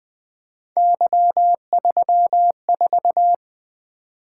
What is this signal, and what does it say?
Morse code, heard as a keyed tone, about 20 wpm, 700 Hz. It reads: Y34
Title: Three Letter-Number Combinations - 20wpm